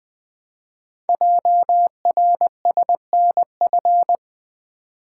Morse code, heard as a keyed tone, about 20 wpm, 700 Hz. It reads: JRSNF
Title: Code Groups - 5 Letters - 20wpm